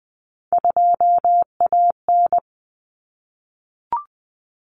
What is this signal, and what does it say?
Morse code, heard as a keyed tone, about 20 wpm, 700 Hz. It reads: 2AN
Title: Common Cut Numbers - 3-Digit 20wpm